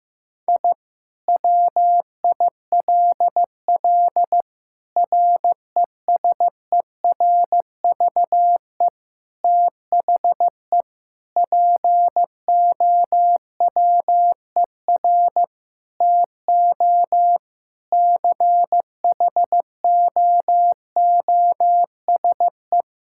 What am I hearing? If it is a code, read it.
Morse code, heard as a keyed tone, about 15 wpm, 700 Hz. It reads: I WILL RESERVE THE POWER TO CHOOSE
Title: Sentences from Top 1000 Words - Review Part 6 - 15wpm